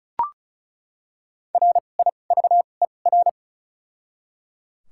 Morse code, heard as a keyed tone, about 34 wpm, spaced as for 25 wpm, 700 Hz. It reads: RIVER
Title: 2x Character Spacing - Top 300 Words - 35wpm